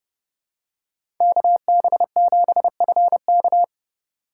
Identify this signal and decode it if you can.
Morse code, heard as a keyed tone, about 30 wpm, 700 Hz. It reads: KB7FK
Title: USA Call Signs - 5 Characters - 4x Character Spacing - 30wpm